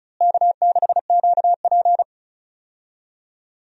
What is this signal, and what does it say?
Morse code, heard as a keyed tone, about 34 wpm, 700 Hz. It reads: K6QP
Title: USA Call Signs - 4 Characters - Repeated 2x - 35wpm